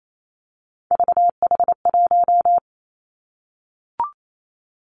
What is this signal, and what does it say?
Morse code, heard as a keyed tone, about 28 wpm, 700 Hz. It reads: VH1
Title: Three Challenging Letter-Number Combinations 28wpm